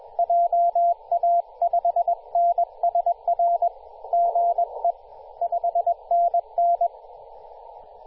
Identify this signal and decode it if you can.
Morse code, heard as a keyed tone, about 21 wpm, 690 Hz, with radio noise. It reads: JA5NSR GE 5NN
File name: t2r-18-071mhz-cw